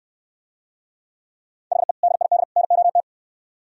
Morse code, heard as a keyed tone, about 67 wpm, 700 Hz. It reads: HE DID NOT